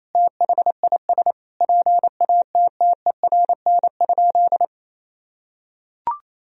Morse code, heard as a keyed tone, about 28 wpm, 700 Hz. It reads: THIS PATTERN?
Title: Sentences from Top 1000 Words - Review Part 2 - 28wpm